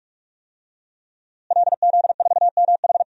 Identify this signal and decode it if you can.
Morse code, heard as a keyed tone, about 46 wpm, 700 Hz. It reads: LZ4MH
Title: DXCC-100 Call Signs - 5 Characters - Repeated 3x - 45wpm